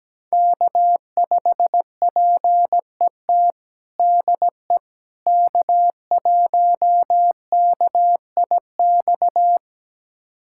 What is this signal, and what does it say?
Morse code, heard as a keyed tone, about 17 wpm, 700 Hz. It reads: K5PET DE K1KIX
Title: USA Callsign DE Callsign - 5 Characters - 17wpm